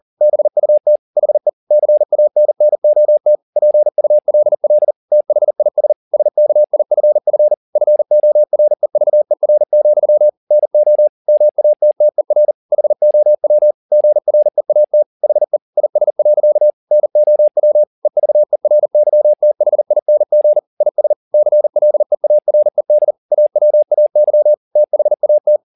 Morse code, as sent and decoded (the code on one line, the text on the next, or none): -... ..- - / .... / -.-. .- -. -. --- - / .--. ..- .-.. .-.. / - .... .. ... / ... -.- .. ..-. ..-. / ..-. --- .-. ...- . .-. --..-- / -. --- / -- .- - - . .-. / .... --- .-- / --. .-. .- - / .... / .. ... .-.-.- / -. --- .-- / . ...- . .-. -.-- - .... .. -. --. / .. ... / -.-. .-.. .- .-. -.. / .- .-- .- -.-- / - .... .- -
BUT HE CANNOT PULL THIS SKIFF FOREVER, NO MATTER HOW GREAT HE IS. NOW EVERYTHING IS CLEARED AWAY THAT